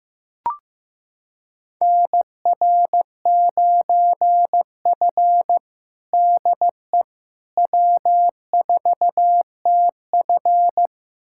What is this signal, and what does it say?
Morse code, heard as a keyed tone, about 15 wpm, 700 Hz. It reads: NR9F DE W4TF
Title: USA Callsign DE Callsign - 4 Characters - 15wpm